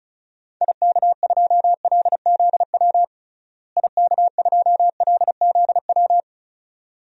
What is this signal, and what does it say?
Morse code, heard as a keyed tone, about 34 wpm, 700 Hz. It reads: IK2LZW IK2LZW
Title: DXCC-100 Call Signs - 6 Characters - Repeated 2x - 35wpm